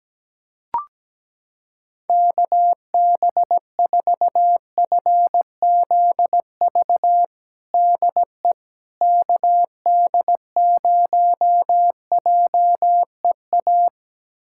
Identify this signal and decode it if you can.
Morse code, heard as a keyed tone, about 17 wpm, 700 Hz. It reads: KB4FZV DE KD0JEA
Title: USA Callsign DE Callsign - 6 Characters - 17wpm